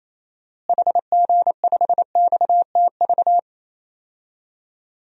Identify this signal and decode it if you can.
Morse code, heard as a keyed tone, about 28 wpm, 700 Hz. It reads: HG5XTV
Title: DXCC-100 Call Signs - 6 Characters - Repeated 2x - 28wpm